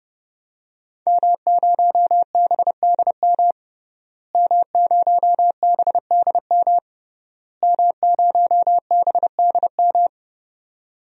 Morse code, heard as a keyed tone, about 32 wpm, 700 Hz. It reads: M0BDM M0BDM M0BDM
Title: DXCC-100 Call Signs - 5 Characters - Repeated 3x - 30wpm